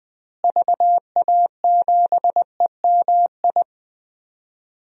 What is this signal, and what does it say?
Morse code, heard as a keyed tone, about 20 wpm, 700 Hz. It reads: VA7EMI
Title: DXCC-100 Call Signs - 6 Characters - Repeated 2x - 20wpm